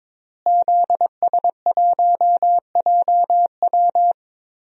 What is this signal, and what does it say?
Morse code, heard as a keyed tone, about 22 wpm, 700 Hz. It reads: ZS1JW
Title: DXCC-100 Call Signs - 5 Characters - Repeated 3x - 22wpm